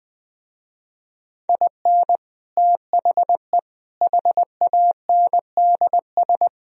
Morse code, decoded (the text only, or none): IN THE HANDS